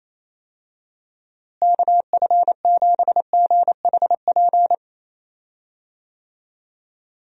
Morse code, heard as a keyed tone, about 28 wpm, 700 Hz. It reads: KF7GHP
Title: USA Call Signs - 6 Characters - Repeated 2x - 28wpm